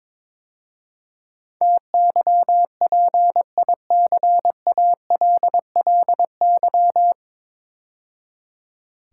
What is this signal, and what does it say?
Morse code, heard as a keyed tone, about 22 wpm, 700 Hz. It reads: TYPICALLY
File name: 9 Letter Words - First 3 Letters Given - 22wpm